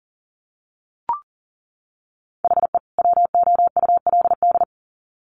Morse code, heard as a keyed tone, about 37 wpm, 700 Hz. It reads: HE WOULD
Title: Sets of 2 Words 40wpm